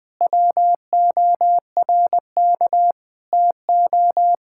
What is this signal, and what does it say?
Morse code, heard as a keyed tone, about 20 wpm, 700 Hz. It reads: WORK TO
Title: Sets of 9 Words - 20wpm